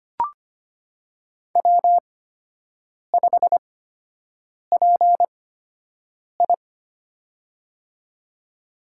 Morse code, heard as a keyed tone, about 25 wpm, spaced as for 7 wpm, 700 Hz. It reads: W5PI
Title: USA Call Signs - 8x Character Spacing - 25wpm